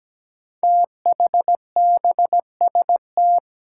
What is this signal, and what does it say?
Morse code, heard as a keyed tone, about 17 wpm, 700 Hz. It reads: THBST
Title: Code Groups - 5 Letters - 17wpm